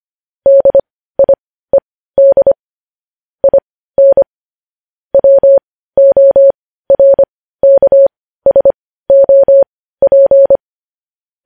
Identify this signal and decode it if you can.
Morse code, heard as a keyed tone, about 25 wpm, spaced as for 15 wpm, 550 Hz. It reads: DIED IN WORKSOP
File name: podcast / News headlines-1x-25x15